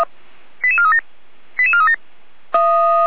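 Five-tone selective call (ZVEI), heard as quick runs of five tones, tones of about 70 ms. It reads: 88437 88437
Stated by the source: ZVEI five-tone call received on an SDR, from the Feuerwehr Orlamünde fire brigade.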